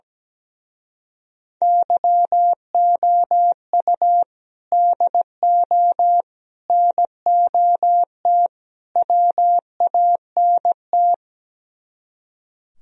Morse code, heard as a keyed tone, about 17 wpm, 700 Hz. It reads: YOU DO NOT WANT
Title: Speed-Racing - Sets of 4 Words 17wpm